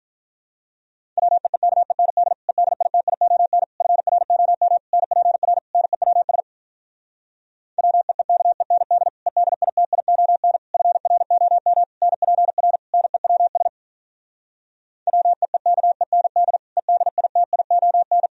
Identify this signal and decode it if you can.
Morse code, heard as a keyed tone, about 47 wpm, 700 Hz. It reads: WEEKEND EDITION FROM NPR NEWS WEEKEND EDITION FROM NPR NEWS WEEKEND EDITION